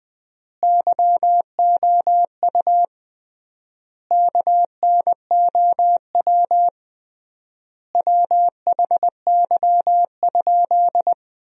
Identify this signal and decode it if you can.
Morse code, heard as a keyed tone, about 20 wpm, 700 Hz. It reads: YOU KNOW WHY?
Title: 3x Word-Spacing - Sentences from Top 200 Words - 20wpm